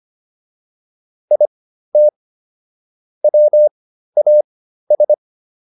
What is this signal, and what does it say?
Morse code, heard as a keyed tone, about 25 wpm, spaced as for 13 wpm, 600 Hz. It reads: IT WAS
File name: WD302 13